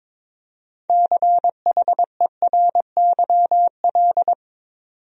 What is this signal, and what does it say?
Morse code, heard as a keyed tone, about 22 wpm, 700 Hz. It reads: CHERYL
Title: USA Common Names - 22wpm